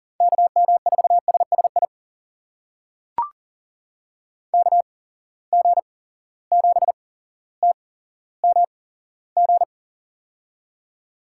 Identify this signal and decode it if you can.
Morse code, heard as a keyed tone, about 40 wpm, spaced as for 17 wpm, 700 Hz. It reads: KM4SSI KG7TMG
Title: USA Call Signs - 6 Characters - 8x Character Spacing - 40wpm